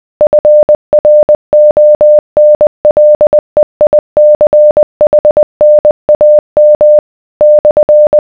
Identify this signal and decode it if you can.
Morse code, heard as a keyed tone, about 20 wpm, 600 Hz. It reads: FRONLEICHNAM /